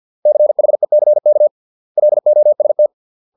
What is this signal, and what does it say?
Morse code, heard as a keyed tone, about 50 wpm, 600 Hz. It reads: CHECK LOST